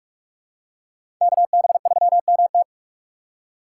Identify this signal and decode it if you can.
Morse code, heard as a keyed tone, about 45 wpm, 700 Hz. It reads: KB3MT
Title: USA Call Signs - 5 Characters - 3x Character Spacing - 45wpm